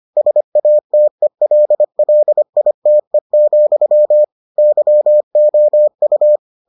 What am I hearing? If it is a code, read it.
Morse code, heard as a keyed tone, about 25 wpm, 600 Hz. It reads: SATELLITE, YOU